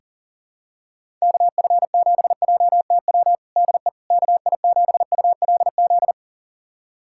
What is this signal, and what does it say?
Morse code, heard as a keyed tone, about 40 wpm, 700 Hz. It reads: KF7JTW DE KI7ULZ
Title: USA Callsign DE Callsign 40wpm